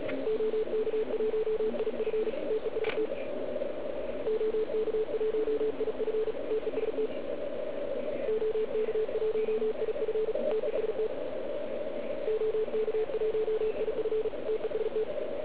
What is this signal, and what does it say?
Morse code, heard as a keyed tone, about 36 wpm, 440 Hz, with radio noise. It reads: OK1IF TEST OK1IF TEST OK1IF TEST OK1IF TEST